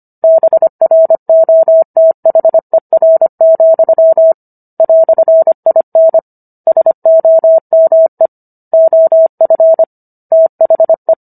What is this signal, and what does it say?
Morse code, heard as a keyed tone, about 25 wpm, 650 Hz. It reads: BROTHER, "IN SOME OF THE